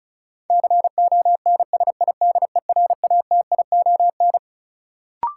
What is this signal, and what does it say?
Morse code, heard as a keyed tone, about 36 wpm, 700 Hz. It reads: CONSIDERATION
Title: Sentences from Top 1500 Words - New Word Drill Alt - 35wpm